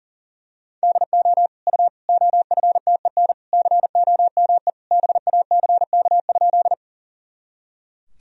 Morse code, heard as a keyed tone, about 40 wpm, 700 Hz. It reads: DO U OFTEN COME BACK?
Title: Sentences from Top 300 Words and Abbreviations 40wpm